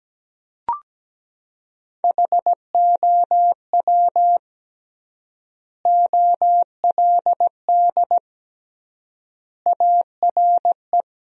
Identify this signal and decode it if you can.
Morse code, heard as a keyed tone, about 17 wpm, 700 Hz. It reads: HOW OLD ARE
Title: 3x Word-Spacing - Sentences from Top 200 Words - 17wpm